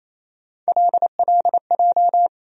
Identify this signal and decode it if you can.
Morse code, heard as a keyed tone, about 28 wpm, 700 Hz. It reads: LLJ
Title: USA Call Sign Postfixes - 3 Letters 28wpm